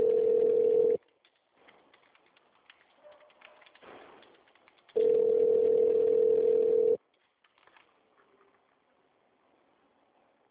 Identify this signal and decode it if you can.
Ringback tone (North America), 440 + 480 Hz.